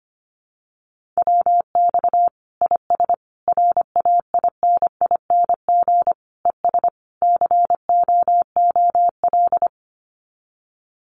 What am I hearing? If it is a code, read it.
Morse code, heard as a keyed tone, about 25 wpm, 700 Hz. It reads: WX IS RAINING ES COOL
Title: Common QSO Phrases 25wpm